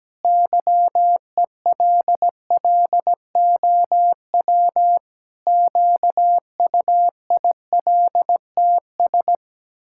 Morse code, heard as a keyed tone, about 17 wpm, 700 Hz. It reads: YELLOW QUILTS
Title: Pangram Challenge v2 - 17wpm